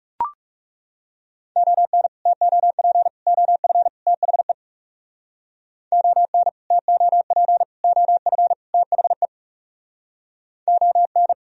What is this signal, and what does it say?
Morse code, heard as a keyed tone, about 37 wpm, 700 Hz. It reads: ON TOP OF THE ON TOP OF THE ON
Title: Speed-Racing - Sets of 4 Words 30wpm